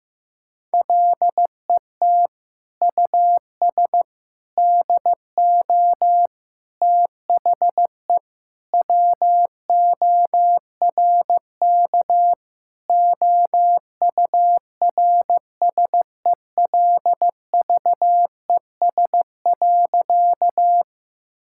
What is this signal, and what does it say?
Morse code, heard as a keyed tone, about 15 wpm, 700 Hz. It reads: LET US DO THE WORK OURSELVES.
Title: Sentences from Top 1600 Words 15wpm